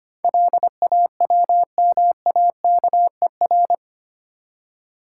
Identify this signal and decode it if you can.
Morse code, heard as a keyed tone, about 25 wpm, 700 Hz. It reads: LAWMAKER
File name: Eight Letter Words - 25wpm